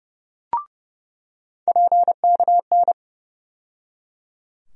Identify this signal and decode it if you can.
Morse code, heard as a keyed tone, about 30 wpm, 700 Hz. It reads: PKN
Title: USA Call Sign Postfixes - 3 Letters 30wpm